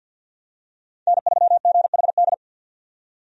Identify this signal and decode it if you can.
Morse code, heard as a keyed tone, about 52 wpm, 700 Hz. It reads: N3GHD
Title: USA Call Signs - 5 Characters - 50wpm